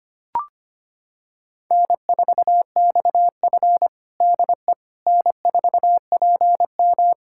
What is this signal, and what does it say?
Morse code, heard as a keyed tone, about 25 wpm, 700 Hz. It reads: N4XF DE N4PM
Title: USA Callsign DE Callsign - 4 Characters - 25wpm